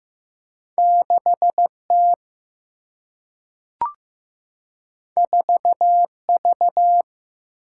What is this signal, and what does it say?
Morse code, heard as a keyed tone, about 15 wpm, 700 Hz. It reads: 6T 4V
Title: Single-Letter Single-Number 15wpm